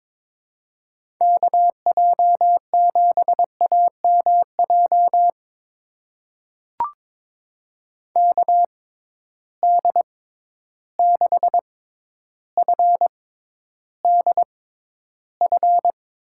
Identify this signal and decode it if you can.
Morse code, heard as a keyed tone, about 22 wpm, spaced as for 11 wpm, 700 Hz. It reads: KJ7AMJ KD6FDF
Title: USA Call Signs - 6 Characters - 6x Character Spacing - 22wpm